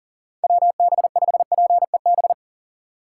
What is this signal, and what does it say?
Morse code, heard as a keyed tone, about 40 wpm, 700 Hz. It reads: WB5PEB
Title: USA Call Signs - 6 Characters - 3x Character Spacing - 40wpm